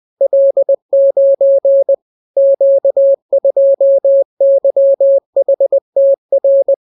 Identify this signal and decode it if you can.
Morse code, heard as a keyed tone, about 20 wpm, 550 Hz. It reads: L9 Q2YHTR